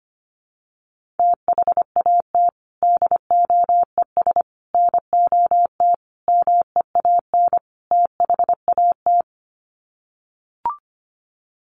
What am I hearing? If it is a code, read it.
Morse code, heard as a keyed tone, about 25 wpm, 700 Hz. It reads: THAT DOES NOT MEAN THAT
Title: Sets of 5 Words Encore 25wpm